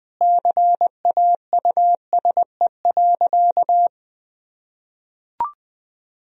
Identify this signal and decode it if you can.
Morse code, heard as a keyed tone, about 21 wpm, 700 Hz. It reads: CAUSE.
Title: Sentences from Top 1000 Words - Review Part 1 - 20wpm